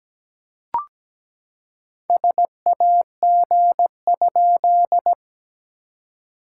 Morse code, heard as a keyed tone, about 17 wpm, 700 Hz. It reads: SAG?
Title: SOTA QSO Practice 17wpm